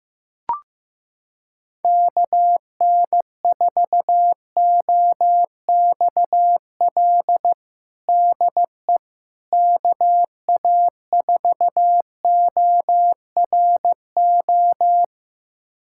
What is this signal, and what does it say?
Morse code, heard as a keyed tone, about 15 wpm, 700 Hz. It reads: KN4OXL DE KA4ORO
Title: USA Callsign DE Callsign 15wpm